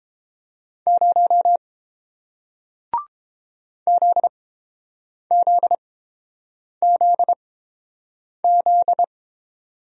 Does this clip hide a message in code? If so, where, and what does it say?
Yes: Morse code from 0.9 to 9.0 s, heard as a keyed tone, about 30 wpm, spaced as for 7 wpm, 690 Hz. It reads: 0 ZZZZ